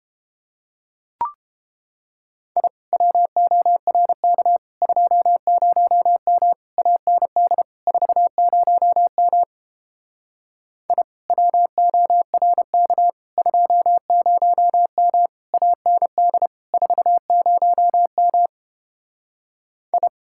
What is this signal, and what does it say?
Morse code, heard as a keyed tone, about 31 wpm, 700 Hz. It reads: I WORK 20M AND 40M I WORK 20M AND 40M I